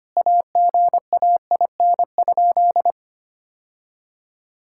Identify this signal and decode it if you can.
Morse code, heard as a keyed tone, about 25 wpm, 700 Hz. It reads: AGAIN?
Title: Sentences from Top 1000 Words - Review Part 6 - 25wpm